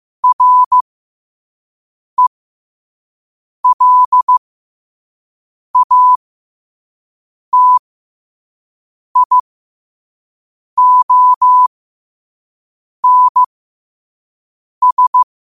Morse code, heard as a keyed tone, about 15 wpm, spaced as for 5 wpm, 1000 Hz. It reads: RELATIONS
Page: New quotes every day in morse code at 5 Words per minute.